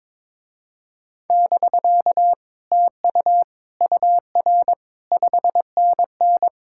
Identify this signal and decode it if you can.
Morse code, heard as a keyed tone, about 22 wpm, 700 Hz. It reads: <BK> TU UR 5NN
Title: Speed-Racing Common QSO Phrases - 22wpm